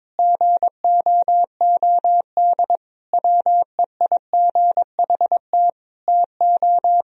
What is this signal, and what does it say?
Morse code, heard as a keyed tone, about 22 wpm, 700 Hz. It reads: GOOD WEIGHT TO
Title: Sentences from Top 1000 Words - Review Part 3 - 22wpm